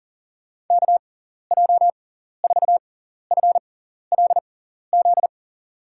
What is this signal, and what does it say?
Morse code, heard as a keyed tone, about 37 wpm, spaced as for 14 wpm, 700 Hz. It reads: KJ4FLZ
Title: USA Call Signs - 6x Character Spacing - 40wpm